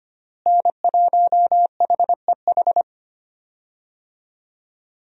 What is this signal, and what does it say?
Morse code, heard as a keyed tone, about 25 wpm, 700 Hz. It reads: N1HEH
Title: USA Call Signs - 5 Characters - Repeated 2x - 25wpm